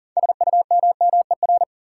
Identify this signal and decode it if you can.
Morse code, heard as a keyed tone, about 40 wpm, 700 Hz. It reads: SUMMER
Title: Sentences from Top 1100 Words - New Word Drill Alt - 40wpm